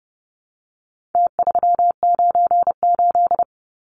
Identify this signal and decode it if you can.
Morse code, heard as a keyed tone, about 30 wpm, 700 Hz. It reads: T398
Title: Numbers - Contest Style - 4-Digit 30wpm